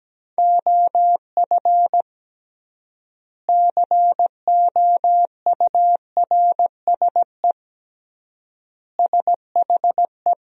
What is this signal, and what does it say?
Morse code, heard as a keyed tone, about 17 wpm, 700 Hz. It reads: OF COURSE SHE
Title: 3x Word-Spacing - Sentences from Top 500 Words - 17wpm